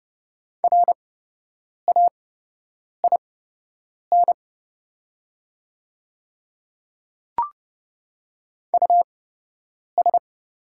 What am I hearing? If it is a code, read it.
Morse code, heard as a keyed tone, about 30 wpm, spaced as for 8 wpm, 700 Hz. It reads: RAIN US
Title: 8x Character Spacing - Top 300 Words - 30wpm